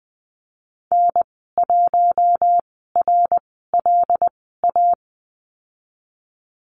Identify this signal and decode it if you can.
Morse code, heard as a keyed tone, about 20 wpm, spaced as for 15 wpm, 700 Hz. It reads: N1RLA
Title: 082 Lesson 14g - Number 2 - Call Signs 20wpm